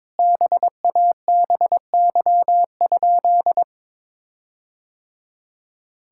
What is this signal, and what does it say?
Morse code, heard as a keyed tone, about 22 wpm, 700 Hz. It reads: BABY?
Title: Sentences from Top 1600 Words 22wpm